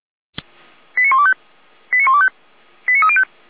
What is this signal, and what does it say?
Five-tone selective call (ZVEI), heard as quick runs of five tones, tones of about 70 ms. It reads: 89125 89125 89305
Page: /455/SDR/ZVEI/